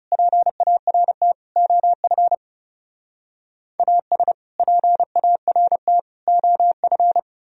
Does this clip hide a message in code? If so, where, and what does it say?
Yes: Morse code from 0.1 to 7.2 s, heard as a keyed tone, about 38 wpm, 700 Hz. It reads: PART OF AS PART OF